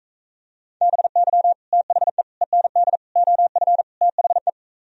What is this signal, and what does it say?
Morse code, heard as a keyed tone, about 44 wpm, 700 Hz. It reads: BY THE END OF THE